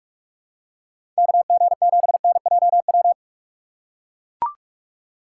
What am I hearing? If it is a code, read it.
Morse code, heard as a keyed tone, about 41 wpm, 700 Hz. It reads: KG7NJW
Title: USA Call Signs - 6x Character Spacing - 45wpm